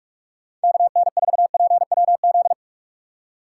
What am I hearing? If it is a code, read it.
Morse code, heard as a keyed tone, about 46 wpm, 700 Hz. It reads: KN4PWZ